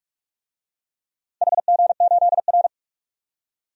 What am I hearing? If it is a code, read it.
Morse code, heard as a keyed tone, about 42 wpm, 700 Hz. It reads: HG8R
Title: DXCC-100 Call Signs - 4 Characters - 45wpm